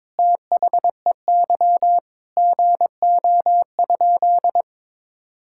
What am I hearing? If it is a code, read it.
Morse code, heard as a keyed tone, about 21 wpm, 700 Hz. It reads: THEY GO?
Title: Sentences from Top 1600 Words 22wpm